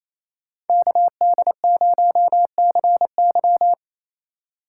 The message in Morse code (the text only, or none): KD0CY